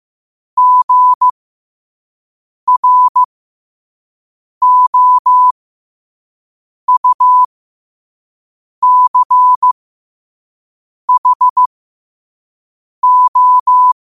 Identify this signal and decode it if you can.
Morse code, heard as a keyed tone, about 15 wpm, spaced as for 5 wpm, 1000 Hz. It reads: GROUCHO
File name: qotdfiles / 5wpm